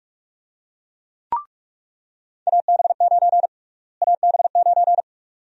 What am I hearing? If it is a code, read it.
Morse code, heard as a keyed tone, about 47 wpm, 700 Hz. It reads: AB9 AB9
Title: USA Call Sign Prefixes - Repeated 2x - 45wpm